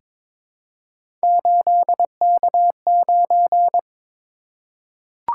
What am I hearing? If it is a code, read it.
Morse code, heard as a keyed tone, about 22 wpm, 700 Hz. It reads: 8K9
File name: practice / 22wpm / Three Letter-Number Combinations - 22wpm